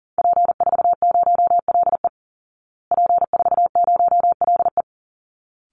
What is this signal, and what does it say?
Morse code, heard as a keyed tone, about 40 wpm, 700 Hz. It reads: P40LE P40LE
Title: Call Signs Repeated 3x - 40wpm